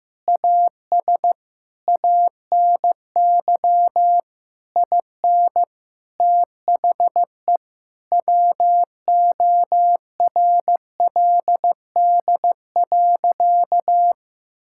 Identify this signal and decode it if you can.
Morse code, heard as a keyed tone, about 16 wpm, 700 Hz. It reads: AS ANY IN THE WORLD.
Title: Sentences from Top 1600 Words 15wpm